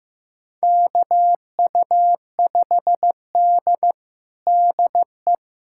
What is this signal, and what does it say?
Morse code, heard as a keyed tone, about 15 wpm, 700 Hz. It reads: KU5D DE
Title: USA Callsign DE Callsign - 4 Characters - 15wpm